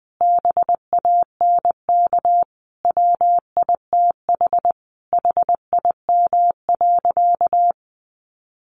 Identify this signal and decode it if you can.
Morse code, heard as a keyed tone, about 20 wpm, 700 Hz. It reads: BANK WITH HIM.
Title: Sentences from Top 900 Words 20wpm